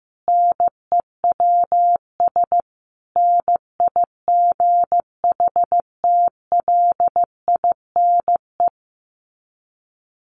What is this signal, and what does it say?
Morse code, heard as a keyed tone, about 15 wpm, 700 Hz. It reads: NEWS NIGHTLINE
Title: Sets of 5 Words 15wpm